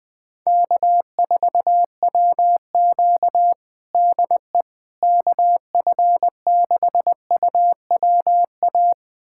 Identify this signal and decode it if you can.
Morse code, heard as a keyed tone, about 20 wpm, 700 Hz. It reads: K4WQ DE KF6UWA
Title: USA Callsign DE Callsign 20wpm